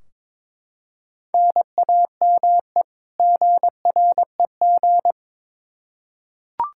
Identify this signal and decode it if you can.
Morse code, heard as a keyed tone, about 22 wpm, 700 Hz. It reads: NAME GREG
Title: Practice QSO - Exchange 1 - Beginning - 22wpm